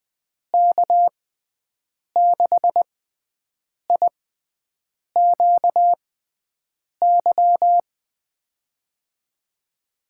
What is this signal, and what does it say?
Morse code, heard as a keyed tone, about 20 wpm, spaced as for 7 wpm, 700 Hz. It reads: K6IQY
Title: USA Call Signs - 6x Character Spacing - 20wpm